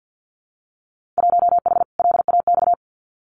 Morse code, heard as a keyed tone, about 50 wpm, 700 Hz. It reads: 1H LAX
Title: Winter Field Day Exchanges 50wpm